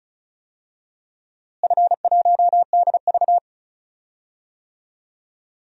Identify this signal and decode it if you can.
Morse code, heard as a keyed tone, about 36 wpm, 700 Hz. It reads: F1DV
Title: DXCC-100 Call Signs - 4 Characters - Repeated 2x - 35wpm